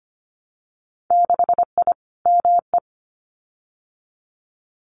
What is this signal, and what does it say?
Morse code, heard as a keyed tone, about 25 wpm, 700 Hz. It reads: 6I ME
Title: Winter Field Day Exchanges Encore 25wpm